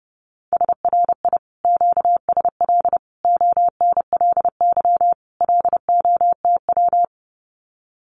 Morse code, heard as a keyed tone, about 30 wpm, 700 Hz. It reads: SRI QSL ONLY LOTW
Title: Common QSO Phrases Encore 30wpm